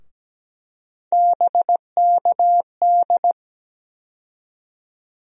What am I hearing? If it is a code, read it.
Morse code, heard as a keyed tone, about 17 wpm, 700 Hz. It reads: BKD
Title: USA Call Sign Postfixes - 3 Letters 17wpm